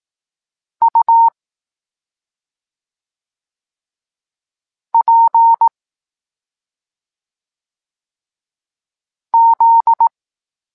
Morse code, heard as a keyed tone, about 18 wpm, 920 Hz. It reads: U P Z